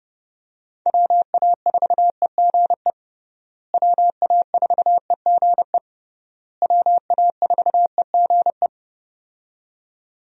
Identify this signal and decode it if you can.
Morse code, heard as a keyed tone, about 32 wpm, 700 Hz. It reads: WA4EGE WA4EGE WA4EGE
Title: USA Call Signs - 6 Characters - Repeated 3x - 30wpm